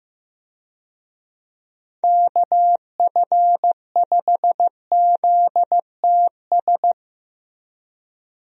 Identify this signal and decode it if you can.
Morse code, heard as a keyed tone, about 15 wpm, 700 Hz. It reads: KF5ZTS
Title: USA Call Signs - 6 Characters - Repeated 3x - 15wpm